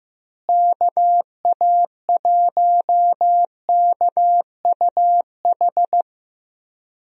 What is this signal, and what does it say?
Morse code, heard as a keyed tone, about 15 wpm, 700 Hz. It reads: KA1KUH
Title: USA Call Signs - 6 Characters - Repeated 3x - 15wpm